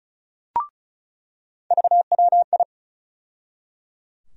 Morse code, heard as a keyed tone, about 34 wpm, 700 Hz. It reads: VWI
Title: USA Call Sign Postfixes - 3 Letters 35wpm